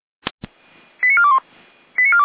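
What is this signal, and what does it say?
Five-tone selective call (ZVEI), heard as quick runs of five tones, tones of about 70 ms. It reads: 89421
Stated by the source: ZVEI five-tone call 89457 received on an SDR, channel 455.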